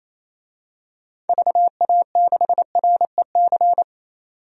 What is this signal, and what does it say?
Morse code, heard as a keyed tone, about 28 wpm, 700 Hz. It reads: VA6REC
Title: DXCC-100 Call Signs - 6 Characters - 28wpm